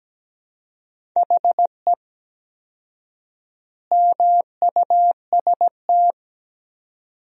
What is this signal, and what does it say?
Morse code, heard as a keyed tone, about 17 wpm, 700 Hz. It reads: HE MUST
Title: 4x Word-Spacing - Sentences from Top 500 Words - 17wpm